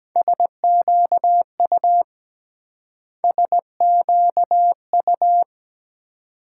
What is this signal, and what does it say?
Morse code, heard as a keyed tone, about 22 wpm, 700 Hz. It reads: SQU SQU